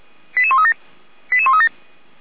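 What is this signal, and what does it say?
Five-tone selective call (ZVEI), heard as quick runs of five tones, tones of about 70 ms. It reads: 88137 88137
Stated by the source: ZVEI five-tone selective call received by SDR.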